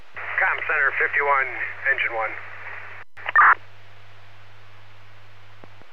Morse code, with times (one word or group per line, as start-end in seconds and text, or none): none